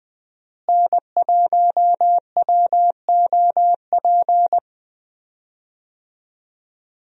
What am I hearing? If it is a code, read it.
Morse code, heard as a keyed tone, about 20 wpm, 700 Hz. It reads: N1WOP
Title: USA Call Signs - 5 Characters - Repeated 3x - 20wpm